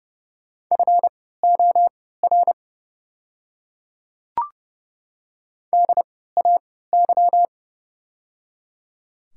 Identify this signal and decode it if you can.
Morse code, heard as a keyed tone, about 30 wpm, spaced as for 17 wpm, 700 Hz. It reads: FOR DAY
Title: 3x Character Spacing - Top 200 Words - 30wpm